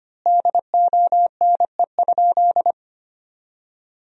Morse code, heard as a keyed tone, about 25 wpm, 700 Hz. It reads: DONE?
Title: 2x Word-Spacing - Sentences from Top 500 Words - 25wpm